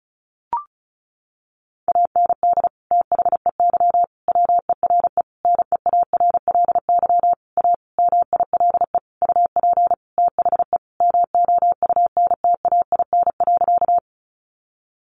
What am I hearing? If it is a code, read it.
Morse code, heard as a keyed tone, about 34 wpm, 700 Hz. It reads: AND THEY WERE NEARLY A MILE UP THE MOUNTAIN.
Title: Sentences from Top 1300 Words 35wpm